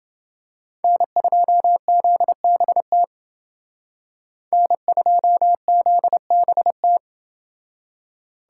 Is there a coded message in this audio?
Yes: Morse code from 0.8 to 7.0 s, heard as a keyed tone, about 29 wpm, 700 Hz. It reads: N2ZBT N2ZBT